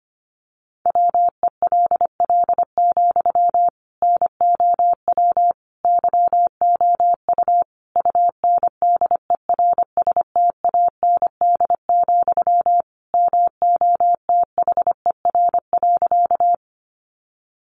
Morse code, heard as a keyed tone, about 25 wpm, 700 Hz. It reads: WELL, NOW YOU UNDERSTAND, MOTHER.
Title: Sentences from Top 600 Words 25wpm